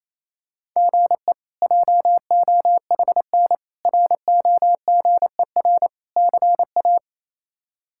Morse code, heard as a keyed tone, about 28 wpm, 700 Hz. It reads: GE JOHN ROGER CA
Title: K1USN SST QSO Practice - 28wpm